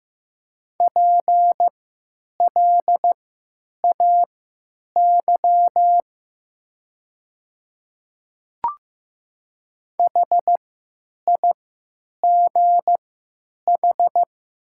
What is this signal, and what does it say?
Morse code, heard as a keyed tone, about 15 wpm, spaced as for 9 wpm, 700 Hz. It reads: PLAY HIGH
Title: 3x Character Spacing - Top 200 Words - 15wpm